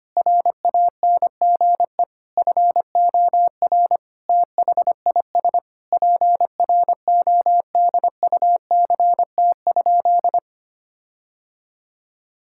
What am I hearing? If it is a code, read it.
Morse code, heard as a keyed tone, about 25 wpm, 700 Hz. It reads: RANGE FOR THIS PRODUCT?
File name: practice / 25wpm / Sentences with 9-Word Phrases - 25wpm